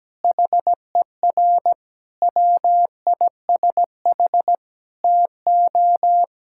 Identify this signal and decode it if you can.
Morse code, heard as a keyed tone, about 17 wpm, 700 Hz. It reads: HER WISH TO
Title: Sentences from Top 1600 Words 17wpm